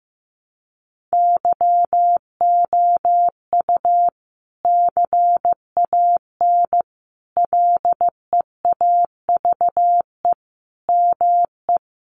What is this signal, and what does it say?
Morse code, heard as a keyed tone, about 15 wpm, 700 Hz. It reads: YOU CAN LEAVE ME
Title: Sentences from Top 1200 Words 15wpm